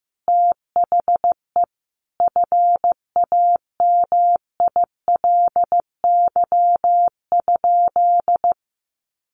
Morse code, heard as a keyed tone, about 15 wpm, 700 Hz. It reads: THE FAMILY?
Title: Sentences from Top 1100 Words 15wpm